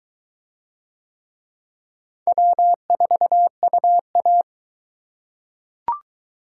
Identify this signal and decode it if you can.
Morse code, heard as a keyed tone, about 21 wpm, 700 Hz. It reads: W4UA